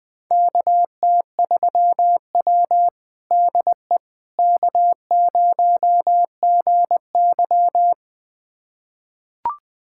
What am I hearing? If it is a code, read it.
Morse code, heard as a keyed tone, about 21 wpm, 700 Hz. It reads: KT3W DE K0GY
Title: USA Callsign DE Callsign - 4 Characters - 20wpm